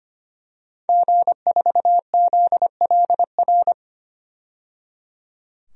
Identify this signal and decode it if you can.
Morse code, heard as a keyed tone, about 25 wpm, 700 Hz. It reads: G4ZLR
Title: DXCC-100 Call Signs - 5 Characters - 25wpm